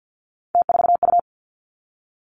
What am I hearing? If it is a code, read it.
Morse code, heard as a keyed tone, about 50 wpm, 700 Hz. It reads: T4U
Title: Three Challenging Letter-Number Combinations 50wpm